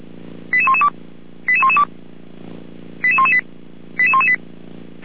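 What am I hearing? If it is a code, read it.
Five-tone selective call (ZVEI), heard as quick runs of five tones, tones of about 70 ms. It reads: 88112 88112 88118 88118
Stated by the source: ZVEI five-tone selective call received on an SDR.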